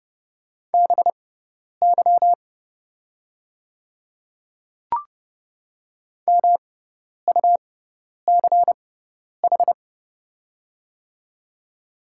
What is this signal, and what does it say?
Morse code, heard as a keyed tone, about 30 wpm, spaced as for 10 wpm, 700 Hz. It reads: BY MUCH
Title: 6x Character Spacing - Top 200 Words - 30wpm